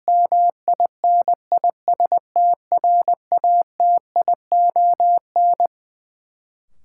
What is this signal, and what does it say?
Morse code, heard as a keyed tone, about 20 wpm, 700 Hz. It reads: MINISTRATION
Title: Sets of 5 Words - New Word Drill - 20wpm